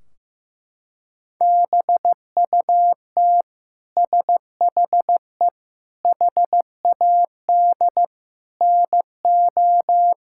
Morse code, heard as a keyed tone, about 16 wpm, 700 Hz. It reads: BUT SHE HAD NO
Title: Sentences from Top 1600 Words 15wpm